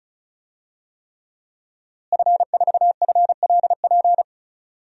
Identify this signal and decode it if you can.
Morse code, heard as a keyed tone, about 34 wpm, 700 Hz. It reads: F4FLP
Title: DXCC-100 Call Signs - 5 Characters - Repeated 3x - 35wpm